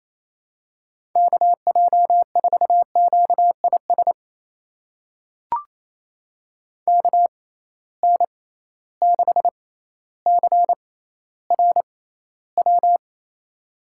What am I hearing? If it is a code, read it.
Morse code, heard as a keyed tone, about 28 wpm, spaced as for 14 wpm, 700 Hz. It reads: KJ4QIS KN6CRW
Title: USA Call Signs - 6 Characters - 6x Character Spacing - 28wpm